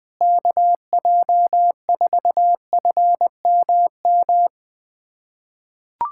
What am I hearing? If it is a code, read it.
Morse code, heard as a keyed tone, about 20 wpm, 700 Hz. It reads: KJ4FMM
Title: USA Callsign DE Callsign - 6 Characters - 20wpm